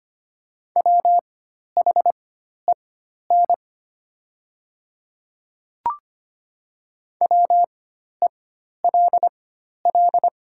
Morse code, heard as a keyed tone, about 25 wpm, spaced as for 12 wpm, 700 Hz. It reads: WHEN WELL
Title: 4x Character Spacing - Top 200 Words - 25wpm